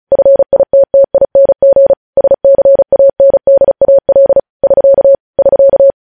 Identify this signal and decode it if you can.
Morse code, heard as a keyed tone, about 36 wpm, 550 Hz. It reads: FITTING SCANDAL <SK> <SK>
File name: podcast / News headlines-1x-35x35